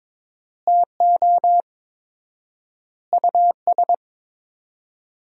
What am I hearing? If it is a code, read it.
Morse code, heard as a keyed tone, about 22 wpm, 700 Hz. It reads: TO US
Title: 4x Word-Spacing - Sentences from Top 500 Words - 22wpm